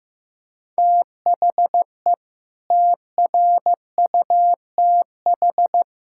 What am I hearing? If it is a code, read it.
Morse code, heard as a keyed tone, about 15 wpm, 700 Hz. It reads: THE TRUTH
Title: Sentences from Top 1600 Words 15wpm